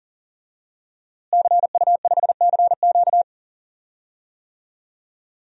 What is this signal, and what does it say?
Morse code, heard as a keyed tone, about 40 wpm, 700 Hz. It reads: CU5CQ
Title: DXCC-100 Call Signs - 5 Characters - Repeated 3x - 40wpm